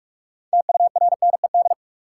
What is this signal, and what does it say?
Morse code, heard as a keyed tone, about 44 wpm, 700 Hz. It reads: TURNED
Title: Sets of 4 Words - New Word Drill - 45wpm